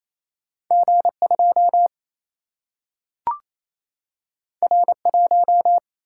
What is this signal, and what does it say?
Morse code, heard as a keyed tone, about 28 wpm, 700 Hz. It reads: G2 R1
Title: Single-Letter Single-Number 28wpm